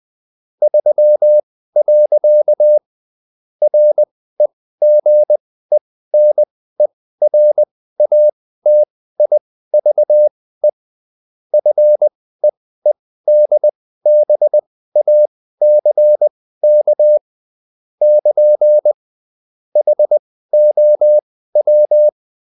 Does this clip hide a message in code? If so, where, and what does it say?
Yes: Morse code from 0.6 to 22.1 s, heard as a keyed tone, about 20 wpm, spaced as for 14 wpm, 600 Hz. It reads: 3. REGENERATIVE FEEDBACK <KN> HOW